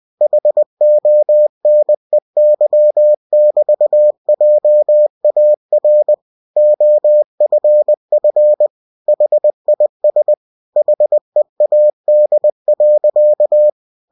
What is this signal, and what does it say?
Morse code, heard as a keyed tone, about 20 wpm, 600 Hz. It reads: HONEY<BT>JAR OFF HIS HEAD.
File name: cw-books / winnie-the-pooh / winnie-20wpm-page-0042